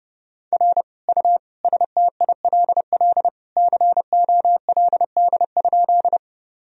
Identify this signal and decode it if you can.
Morse code, heard as a keyed tone, about 30 wpm, 700 Hz. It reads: R U STILL COLD?
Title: Sentences from Top 400 Words and Abbreviations 30wpm